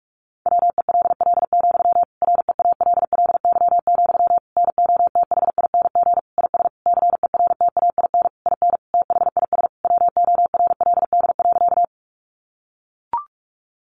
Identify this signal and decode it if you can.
Morse code, heard as a keyed tone, about 44 wpm, 700 Hz. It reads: WELL, REALLY, NOTHING IS CERTAIN IN THIS WORLD.
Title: Sentences from Top 600 Words 45wpm